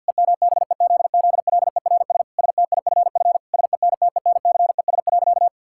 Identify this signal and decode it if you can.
Morse code, heard as a keyed tone, about 52 wpm, 700 Hz. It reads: EMBEZZLERS STIFF SENTENCES.
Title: Pangram Challenge v2 - 50wpm